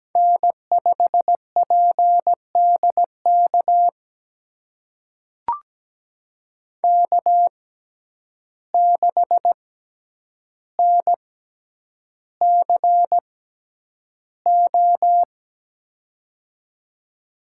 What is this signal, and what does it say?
Morse code, heard as a keyed tone, about 17 wpm, spaced as for 9 wpm, 700 Hz. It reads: N5PDK K6NCO
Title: USA Call Signs - 5 Characters - 6x Character Spacing - 17wpm